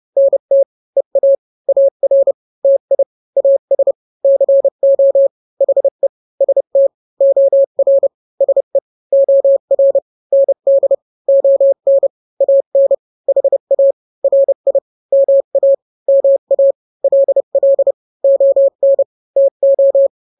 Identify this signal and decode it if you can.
Morse code, heard as a keyed tone, about 30 wpm, 550 Hz. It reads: NT EA AR TI AS CO HE ST OR SE OR ND ON AN HA RI MA MA LL ON TO
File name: n-grams / 2-grams 30 25 WPM